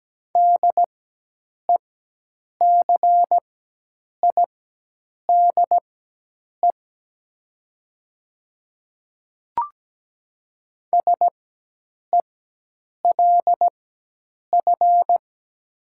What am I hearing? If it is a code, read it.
Morse code, heard as a keyed tone, about 17 wpm, spaced as for 8 wpm, 700 Hz. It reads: DECIDE SELF
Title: 4x Character Spacing - Top 500 Words - 17wpm